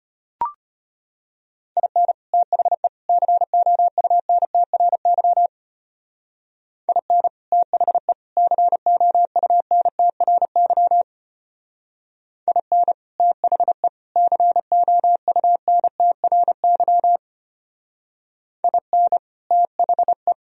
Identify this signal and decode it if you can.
Morse code, heard as a keyed tone, about 31 wpm, 700 Hz. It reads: IN THE COUNTRY IN THE COUNTRY IN THE COUNTRY IN THE
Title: Speed-Racing - Sets of 3 Words 25wpm